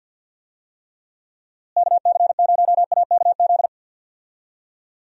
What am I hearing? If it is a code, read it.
Morse code, heard as a keyed tone, about 46 wpm, 700 Hz. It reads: KC0AKZ
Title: USA Call Signs - 6 Characters - Repeated 2x - 50wpm